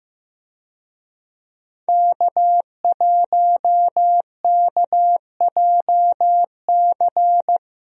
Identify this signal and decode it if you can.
Morse code, heard as a keyed tone, about 15 wpm, 700 Hz. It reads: K1KJC
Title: USA Call Signs - 5 Characters - Repeated 3x - 15wpm